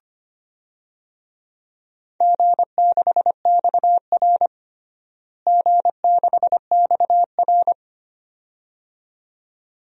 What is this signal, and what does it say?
Morse code, heard as a keyed tone, about 25 wpm, 700 Hz. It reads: G6XR G6XR
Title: DXCC-100 Call Signs - 4 Characters - Repeated 2x - 25wpm